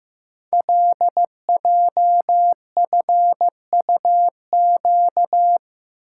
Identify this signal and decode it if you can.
Morse code, heard as a keyed tone, about 15 wpm, 700 Hz. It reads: LJFUQ
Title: Code Groups - 5 Letters - 15wpm